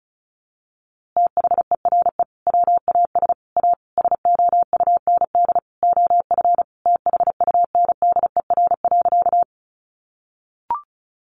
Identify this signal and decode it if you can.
Morse code, heard as a keyed tone, about 34 wpm, 700 Hz. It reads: THERE WAS A SOUND OF THUNDER.
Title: A Sound Of Thunder 35wpm